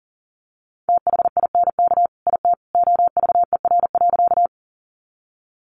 Morse code, heard as a keyed tone, about 40 wpm, 700 Hz. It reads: THINK IT OVER.
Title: Sentences from Top 200 Words 40wpm